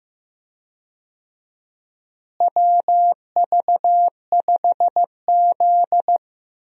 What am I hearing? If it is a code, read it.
Morse code, heard as a keyed tone, about 15 wpm, 700 Hz. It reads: WV5Z
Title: USA Call Signs - 4 Characters - Repeated 3x - 15wpm